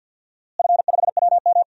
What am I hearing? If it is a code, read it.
Morse code, heard as a keyed tone, about 52 wpm, 700 Hz. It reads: F5WM
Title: DXCC-100 Call Signs - 4 Characters - Repeated 3x - 50wpm